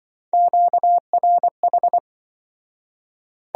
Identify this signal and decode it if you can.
Morse code, heard as a keyed tone, about 24 wpm, 700 Hz. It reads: QRH
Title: Speed-Racing Q-Codes - 20wpm